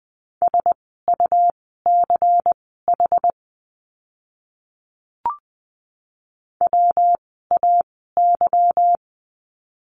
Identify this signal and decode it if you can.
Morse code, heard as a keyed tone, about 20 wpm, spaced as for 15 wpm, 700 Hz. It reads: SUCH WAY
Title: 146 Lesson 24f - Letter y - Words 20wpm